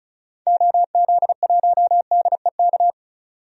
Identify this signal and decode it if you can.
Morse code, heard as a keyed tone, about 36 wpm, 700 Hz. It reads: OZ1DEK
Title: DXCC-100 Call Signs - 6 Characters - 35wpm